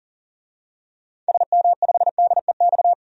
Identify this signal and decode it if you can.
Morse code, heard as a keyed tone, about 40 wpm, 700 Hz. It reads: SM5DEX
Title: DXCC-100 Call Signs - 6 Characters - 40wpm